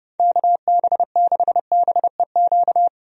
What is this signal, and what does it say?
Morse code, heard as a keyed tone, about 30 wpm, 700 Hz. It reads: KB6BEQ
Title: USA Call Signs - 6 Characters - 30wpm